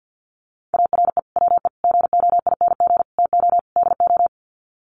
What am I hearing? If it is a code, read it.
Morse code, heard as a keyed tone, about 50 wpm, 700 Hz. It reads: ARE WE GOING TO DO
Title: Sets of 5 Words 50wpm